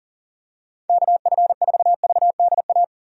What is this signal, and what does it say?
Morse code, heard as a keyed tone, about 40 wpm, 700 Hz. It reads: KF4VDA
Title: USA Call Signs - 6 Characters - Repeated 2x - 40wpm